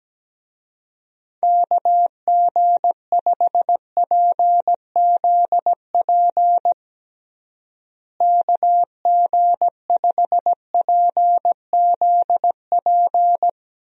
Morse code, heard as a keyed tone, about 17 wpm, 700 Hz. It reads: KG5PZP KG5PZP
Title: USA Call Signs - 6 Characters - Repeated 3x - 17wpm